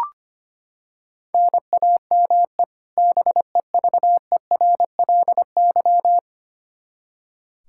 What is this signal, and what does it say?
Morse code, heard as a keyed tone, about 25 wpm, 700 Hz. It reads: NAME BEVERLY
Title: Practice QSO - Exchange 1 - Beginning - 25wpm